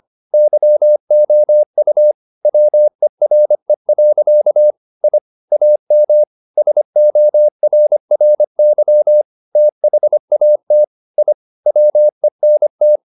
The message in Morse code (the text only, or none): YOU WERE. I AM SORRY THAT I WENT